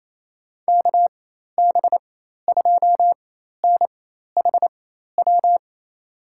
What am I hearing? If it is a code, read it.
Morse code, heard as a keyed tone, about 28 wpm, spaced as for 13 wpm, 700 Hz. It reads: KB2NHW
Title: USA Call Signs - 6 Characters - 4x Character Spacing - 28wpm